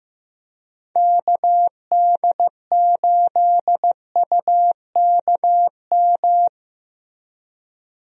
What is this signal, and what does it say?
Morse code, heard as a keyed tone, about 15 wpm, 700 Hz. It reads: KD8UKM
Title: USA Call Signs - 6 Characters - Repeated 2x - 15wpm